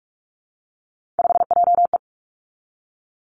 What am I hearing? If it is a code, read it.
Morse code, heard as a keyed tone, about 46 wpm, 700 Hz. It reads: 5JE
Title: Three Challenging Letter-Number Combinations 45wpm